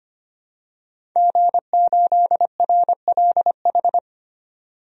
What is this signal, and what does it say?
Morse code, heard as a keyed tone, about 25 wpm, 700 Hz. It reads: G8RLH
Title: DXCC-100 Call Signs - 5 Characters - 25wpm